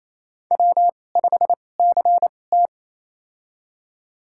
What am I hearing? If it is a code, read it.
Morse code, heard as a keyed tone, about 28 wpm, spaced as for 20 wpm, 700 Hz. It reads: W5CT
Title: USA Call Signs - 4 Characters - 2x Character Spacing - 28wpm